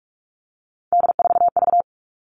Morse code, heard as a keyed tone, about 44 wpm, 700 Hz. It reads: D4V
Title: All Cut Numbers - 3-Digit 45wpm